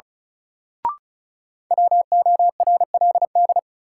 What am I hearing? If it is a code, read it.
Morse code, heard as a keyed tone, about 36 wpm, 700 Hz. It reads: WORLD
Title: Sets of 2 Words - New Word Drill - 35wpm